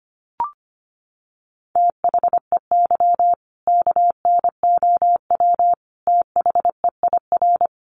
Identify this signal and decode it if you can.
Morse code, heard as a keyed tone, about 25 wpm, 700 Hz. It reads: THEY KNOW THEIR
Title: Sentences from Top 1200 Words 25wpm